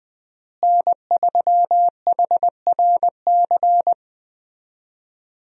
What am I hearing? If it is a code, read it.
Morse code, heard as a keyed tone, about 20 wpm, 700 Hz. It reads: N3HRC
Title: USA Call Signs - 5 Characters - Repeated 3x - 20wpm